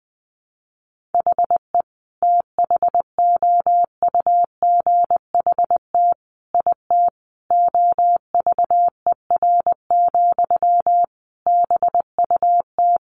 Morse code, heard as a keyed tone, about 20 wpm, 700 Hz. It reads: HE THOUGHT IT OVER, BUT
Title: Sentences from Top 1200 Words 20wpm